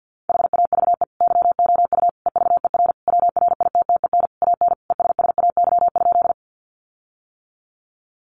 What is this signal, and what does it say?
Morse code, heard as a keyed tone, about 50 wpm, 700 Hz. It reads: HAVE YOU EVER WRITTEN AN ESSAY?
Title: Sentences from Top 1500 Words 50wpm